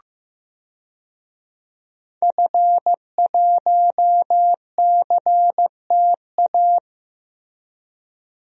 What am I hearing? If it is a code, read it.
Morse code, heard as a keyed tone, about 15 wpm, 700 Hz. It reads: F1CTA
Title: DXCC-100 Call Signs - 5 Characters - Repeated 2x - 15wpm